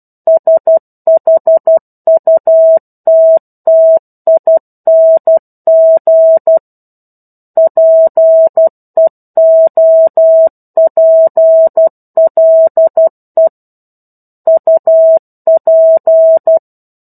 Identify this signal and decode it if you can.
Morse code, heard as a keyed tone, about 12 wpm, 650 Hz. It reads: SHUTTING PEOPLE UP
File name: 01-Chapter-01 12 WPM